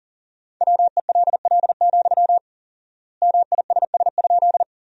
Morse code, heard as a keyed tone, about 40 wpm, 700 Hz. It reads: WELL, MISS?
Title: 4x Word-Spacing - Sentences from Top 500 Words - 40wpm